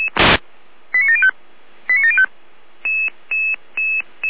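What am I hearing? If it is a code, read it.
Five-tone selective call (ZVEI), heard as quick runs of five tones, tones of about 70 ms. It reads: 89705 89705
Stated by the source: ZVEI five-tone selective call received by SDR.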